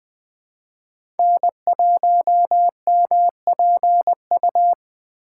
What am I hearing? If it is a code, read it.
Morse code, heard as a keyed tone, about 20 wpm, 700 Hz. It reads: N1MPU
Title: USA Call Signs - 5 Characters - 3x Character Spacing - 20wpm